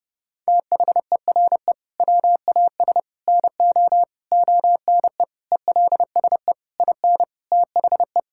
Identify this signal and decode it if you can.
Morse code, heard as a keyed tone, about 30 wpm, 700 Hz. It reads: THERE WAS NO ONE ELSE IN THE
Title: Sentences from Top 1600 Words 30wpm